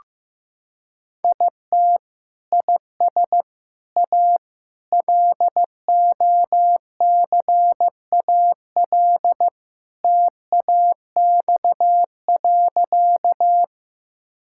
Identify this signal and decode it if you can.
Morse code, heard as a keyed tone, about 16 wpm, 700 Hz. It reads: IT IS A LOCAL TAX.
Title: Sentences from Top 1000 Words - Review Part 5 - 15wpm